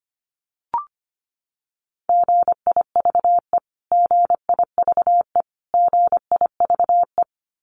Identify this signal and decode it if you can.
Morse code, heard as a keyed tone, about 25 wpm, 700 Hz. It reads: GIVE GIVE GIVE
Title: Top 200 Words Repeated 25wpm